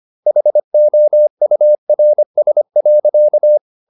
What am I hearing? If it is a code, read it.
Morse code, heard as a keyed tone, about 25 wpm, 600 Hz. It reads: HOURS.
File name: the old man and the sea part 1 25wpm